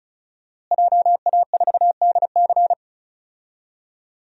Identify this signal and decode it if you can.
Morse code, heard as a keyed tone, about 36 wpm, 700 Hz. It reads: JA4DC
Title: DXCC-100 Call Signs - 5 Characters - Repeated 2x - 35wpm